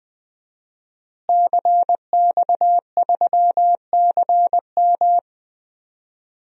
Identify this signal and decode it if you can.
Morse code, heard as a keyed tone, about 20 wpm, 700 Hz. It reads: CX3CM
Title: DXCC-100 Call Signs - 5 Characters - Repeated 3x - 20wpm